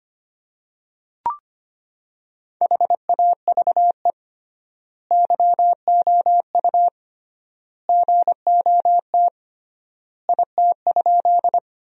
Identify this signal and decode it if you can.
Morse code, heard as a keyed tone, about 25 wpm, 700 Hz. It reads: HAVE YOU GOT IT?
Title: 3x Word-Spacing - Sentences from Top 300 Words - 25wpm